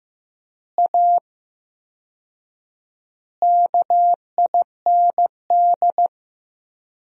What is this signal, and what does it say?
Morse code, heard as a keyed tone, about 15 wpm, 700 Hz. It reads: A KIND
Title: 4x Word-Spacing - Sentences from Top 500 Words - 15wpm